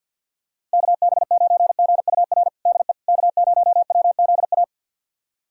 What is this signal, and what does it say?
Morse code, heard as a keyed tone, about 49 wpm, 700 Hz. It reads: KB9GUR DE K0WZA
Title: USA Callsign DE Callsign 50wpm